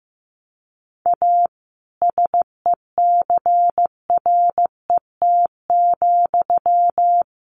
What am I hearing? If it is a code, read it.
Morse code, heard as a keyed tone, about 15 wpm, 700 Hz. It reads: A SECRET,
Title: Sentences from Top 1200 Words 15wpm